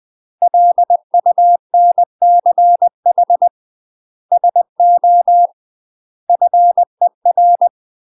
Morse code, heard as a keyed tone, about 20 wpm, 700 Hz. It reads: LUNCH SO FER
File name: UK MASTER QSOs TRAINING / 20WPM / UK MASTER QSOs 0002